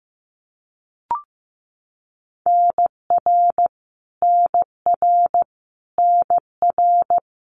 Morse code, heard as a keyed tone, about 15 wpm, 700 Hz. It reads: NR NR NR
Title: Abbreviations Repeated 3x - 15wpm